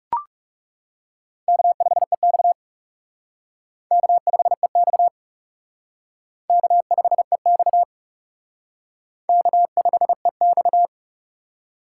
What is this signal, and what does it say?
Morse code, heard as a keyed tone, about 37 wpm, 700 Hz. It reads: K5EX K5EX K5EX K5EX
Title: Speed Racing - USA 1x2 Call Signs 30wpm